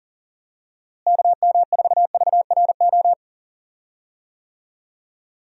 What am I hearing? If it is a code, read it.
Morse code, heard as a keyed tone, about 40 wpm, 700 Hz. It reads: KM4VRO
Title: USA Call Signs - 6 Characters - Repeated 3x - 40wpm